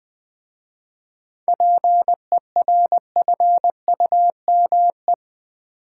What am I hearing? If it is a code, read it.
Morse code, heard as a keyed tone, about 20 wpm, 700 Hz. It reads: PERFUME
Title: Easy Words with Hints - 20wpm